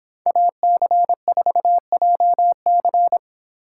Morse code, heard as a keyed tone, about 26 wpm, 700 Hz. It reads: AC4JC